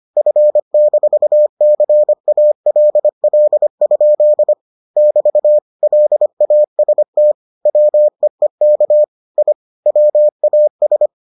Morse code, heard as a keyed tone, about 25 wpm, 600 Hz. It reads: F-CALL? <BT> LAST WEEK I WAS